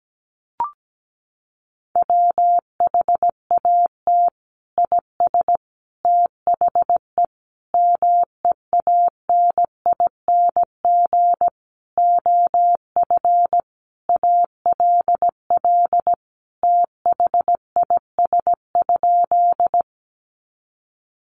Morse code, heard as a keyed tone, about 17 wpm, 700 Hz. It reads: WHAT IS THE MEANING OF ALL THIS?
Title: Sentences from Top 900 Words 17wpm